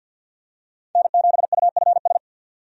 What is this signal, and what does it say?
Morse code, heard as a keyed tone, about 53 wpm, 700 Hz. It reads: N7UFS
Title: USA Call Signs - 5 Characters - 50wpm